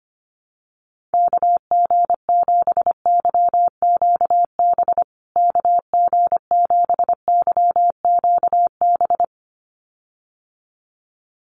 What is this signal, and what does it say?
Morse code, heard as a keyed tone, about 25 wpm, 700 Hz. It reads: KG7YQB KG7YQB
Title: USA Call Signs Repeated 2x 25wpm